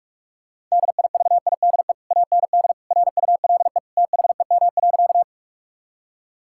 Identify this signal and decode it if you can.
Morse code, heard as a keyed tone, about 46 wpm, 700 Hz. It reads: DIVIDE AND RULE THEM.
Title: Sentences from Top 1600 Words 45wpm